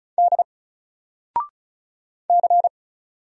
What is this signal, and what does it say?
Morse code, heard as a keyed tone, about 36 wpm, 700 Hz. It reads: D C
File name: Single-Letter-Number-35wpm